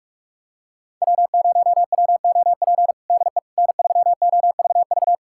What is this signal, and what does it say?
Morse code, heard as a keyed tone, about 46 wpm, 700 Hz. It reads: W0WOP DE N3OVV
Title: USA Callsign DE Callsign - 5 Characters - 45wpm